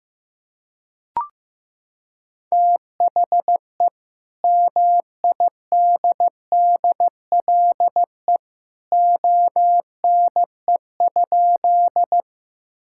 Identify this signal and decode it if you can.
Morse code, heard as a keyed tone, about 15 wpm, 700 Hz. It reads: THE MIDDLE ONE?
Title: Sentences from Top 1000 Words - Review Part 6 - 15wpm